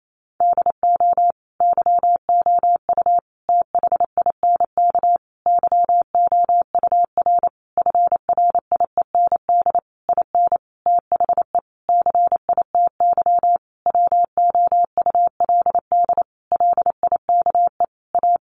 Morse code, heard as a keyed tone, about 28 wpm, 700 Hz. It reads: DO YOU THINK YOUR FRIEND IN THE CITY WOULD LIKE A
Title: Sentences from Top 1100 Words 28wpm